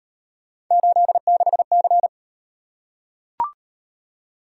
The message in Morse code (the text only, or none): 86C